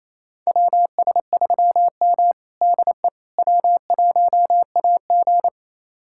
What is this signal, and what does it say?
Morse code, heard as a keyed tone, about 27 wpm, 700 Hz. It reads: WS3M DE W1AG
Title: USA Callsign DE Callsign - 4 Characters - 28wpm